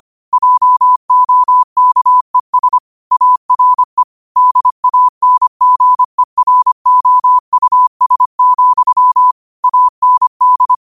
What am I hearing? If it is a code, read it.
Morse code, heard as a keyed tone, about 25 wpm, 1000 Hz. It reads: JOKES ARE DANGEROUS, AND
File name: 25wpm